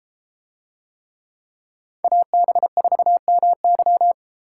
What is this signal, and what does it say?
Morse code, heard as a keyed tone, about 31 wpm, 700 Hz. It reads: AB4MY